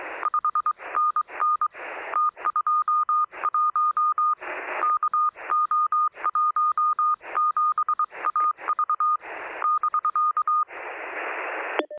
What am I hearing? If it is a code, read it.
Morse code, heard as a keyed tone, about 22 wpm, 1220 Hz. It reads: 5NN T21 KO17AV <BK>
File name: sm7lcb / 0531z es0s 23cm